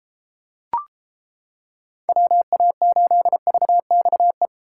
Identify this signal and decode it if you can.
Morse code, heard as a keyed tone, about 31 wpm, 700 Hz. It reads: WA8VXE